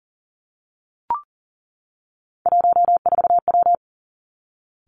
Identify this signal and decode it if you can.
Morse code, heard as a keyed tone, about 42 wpm, 700 Hz. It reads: 14W
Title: Three Challenging Letter-Number Combinations 40wpm